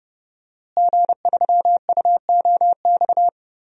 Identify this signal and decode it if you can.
Morse code, heard as a keyed tone, about 30 wpm, 700 Hz. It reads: G3UOX
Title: DXCC-100 Call Signs - 5 Characters - Repeated 2x - 30wpm